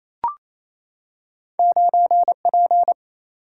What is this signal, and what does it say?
Morse code, heard as a keyed tone, about 28 wpm, 700 Hz. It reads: 9P
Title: Single-Letter Single-Number 28wpm